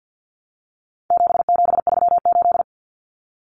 Morse code, heard as a keyed tone, about 50 wpm, 700 Hz. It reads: 7738
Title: Numbers - 4-Digit 50wpm